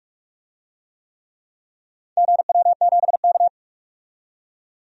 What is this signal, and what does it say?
Morse code, heard as a keyed tone, about 48 wpm, 700 Hz. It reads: GW7K
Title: DXCC-100 Call Signs - 4 Characters - 45wpm